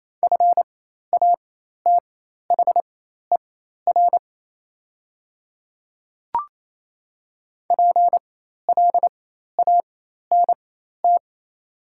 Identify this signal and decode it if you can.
Morse code, heard as a keyed tone, about 28 wpm, spaced as for 13 wpm, 700 Hz. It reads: FATHER PLANT
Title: 4x Character Spacing - Top 300 Words - 28wpm